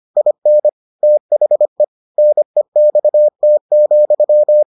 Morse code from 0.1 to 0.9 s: IN

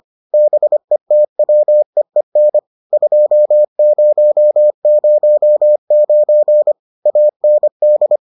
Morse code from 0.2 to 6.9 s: BETWEEN 2009

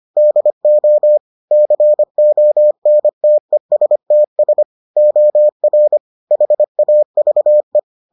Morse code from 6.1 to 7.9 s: HAVE

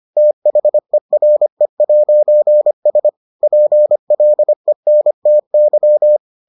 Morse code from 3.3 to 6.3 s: PLENTY

